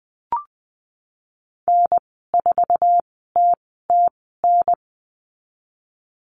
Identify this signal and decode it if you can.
Morse code, heard as a keyed tone, about 20 wpm, spaced as for 15 wpm, 700 Hz. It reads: N4TTN
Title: 048 Lesson 9g - Number 4 - Call Signs 20wpm